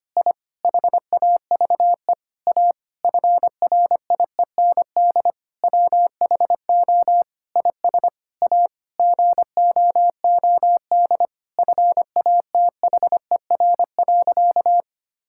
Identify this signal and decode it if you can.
Morse code, heard as a keyed tone, about 25 wpm, 700 Hz. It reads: I HAVE A FRIEND WHO IS A GOOD FATHER.
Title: Sentences from Top 1000 Words - Review Part 5 - 25wpm